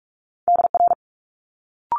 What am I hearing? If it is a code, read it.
Morse code, heard as a keyed tone, about 46 wpm, 700 Hz. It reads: DR
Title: Multi-letter Phonemes Quick Fire 45wpm